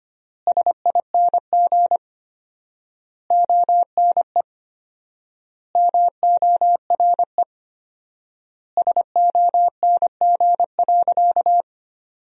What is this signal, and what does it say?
Morse code, heard as a keyed tone, about 25 wpm, 700 Hz. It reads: SING ONE MORE SONG.
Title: 4x Word-Spacing - Sentences from Top 400 Words - 25wpm